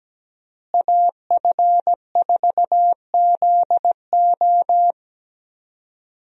Morse code, heard as a keyed tone, about 17 wpm, 700 Hz. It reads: AF4ZO
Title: USA Call Signs - 5 Characters - Repeated 3x - 17wpm